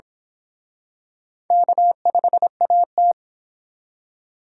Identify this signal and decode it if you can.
Morse code, heard as a keyed tone, about 26 wpm, 700 Hz. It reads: K5AT
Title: Speed Racing - USA 1x2 Call Signs 22wpm